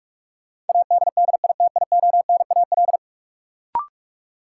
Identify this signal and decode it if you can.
Morse code, heard as a keyed tone, about 46 wpm, 700 Hz. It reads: ADDITIONAL
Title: Sentences from Top 1400 Words - New Word Drill Alt - 45wpm